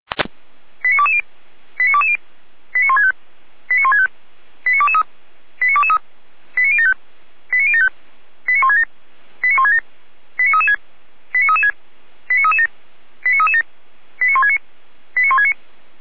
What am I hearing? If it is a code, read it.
Five-tone selective call (ZVEI), heard as quick runs of five tones, tones of about 70 ms. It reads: 89220 89220 89165 89165 89223 89223 89075 89075 89167 89167 89336 89336 89337 89337 89169 89169